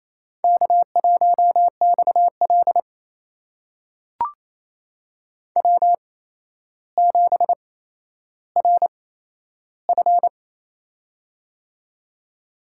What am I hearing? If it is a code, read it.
Morse code, heard as a keyed tone, about 28 wpm, spaced as for 12 wpm, 700 Hz. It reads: K1XL W7RF
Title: USA Call Signs - 4 Characters - 8x Character Spacing - 28wpm